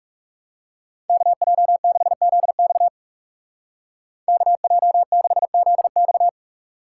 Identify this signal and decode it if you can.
Morse code, heard as a keyed tone, about 37 wpm, 700 Hz. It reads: KJ6ZX KJ6ZX
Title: Speed Racing - USA 2x2 Call Signs 30wpm